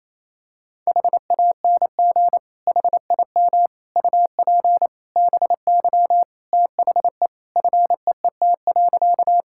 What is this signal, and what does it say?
Morse code, heard as a keyed tone, about 28 wpm, 700 Hz. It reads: HANG HIM UP BY THE FEET.
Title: Sentences from Top 1600 Words 28wpm